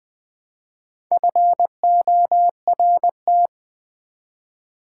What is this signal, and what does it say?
Morse code, heard as a keyed tone, about 20 wpm, 700 Hz. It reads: FORT
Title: Build A Word - 1 Letter At A Time - 5-Letter Words - 20wpm